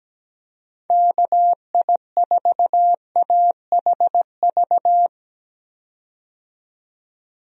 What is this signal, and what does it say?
Morse code, heard as a keyed tone, about 17 wpm, 700 Hz. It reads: KI4AHV
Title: USA Call Signs - 6 Characters - 17wpm